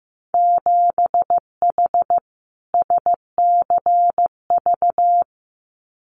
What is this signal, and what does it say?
Morse code, heard as a keyed tone, about 15 wpm, 700 Hz. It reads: 7H SCV
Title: Winter Field Day Exchanges Encore 15wpm